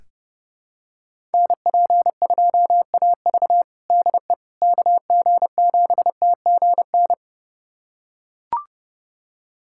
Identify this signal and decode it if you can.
Morse code, heard as a keyed tone, about 30 wpm, 700 Hz. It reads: NP2AV DE KG7TGN
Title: USA Callsign DE Callsign 30wpm